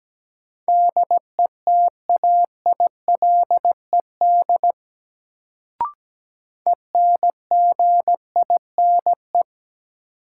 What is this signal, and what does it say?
Morse code, heard as a keyed tone, about 17 wpm, 700 Hz. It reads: DETAILED ENGINE
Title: Sentences from Top 1500 Words - New Word Drill - 17wpm